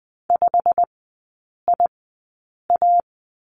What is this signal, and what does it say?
Morse code, heard as a keyed tone, about 20 wpm, spaced as for 8 wpm, 700 Hz. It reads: 5IA
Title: 159 Lesson 26e - Letter g - Three Characters 20wpm